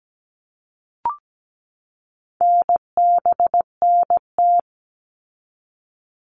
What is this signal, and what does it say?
Morse code, heard as a keyed tone, about 17 wpm, 700 Hz. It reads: NBNT
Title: All Cut Numbers - 4-Digit 17wpm